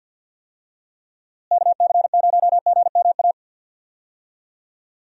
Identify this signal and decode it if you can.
Morse code, heard as a keyed tone, about 51 wpm, 700 Hz. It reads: KC0GMA
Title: USA Call Signs - 6 Characters - Repeated 2x - 50wpm